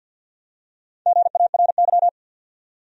Morse code, heard as a keyed tone, about 53 wpm, 700 Hz. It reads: GARY
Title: USA Common Names - Max 5 Letters - 50wpm